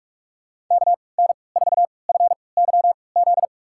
Morse code, heard as a keyed tone, about 46 wpm, spaced as for 25 wpm, 700 Hz. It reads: KN4FYZ
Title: USA Call Signs - 6 Characters - 3x Character Spacing - 45wpm